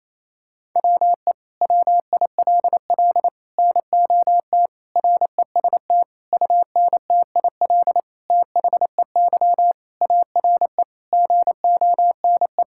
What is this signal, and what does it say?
Morse code, heard as a keyed tone, about 28 wpm, 700 Hz. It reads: WE WILL NOT REST UNTIL THEY ARE GONE
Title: Sentences from Top 1600 Words 28wpm